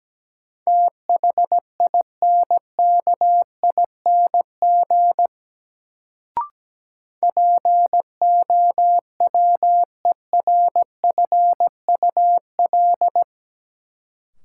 Morse code, heard as a keyed tone, about 17 wpm, 700 Hz. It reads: THINKING POWERFUL
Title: Sentences from Top 1000 Words - Review Part 6 - New Word Drill - 17wpm